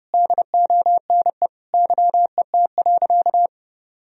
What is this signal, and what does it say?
Morse code, heard as a keyed tone, about 30 wpm, 700 Hz. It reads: DONE YET.
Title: Sentences from Top 1000 Words - Review Part 2 - 30wpm